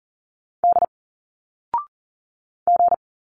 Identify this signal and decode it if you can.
Morse code, heard as a keyed tone, about 39 wpm, 700 Hz. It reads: D G
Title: Single Letters 40wpm